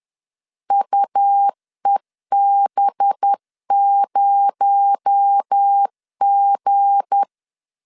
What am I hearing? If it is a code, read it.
Morse code, heard as a keyed tone, about 10 wpm, 800 Hz. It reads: UEB0G
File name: LEZIONI / 7 GF9Z / 7 G 10wpm 2501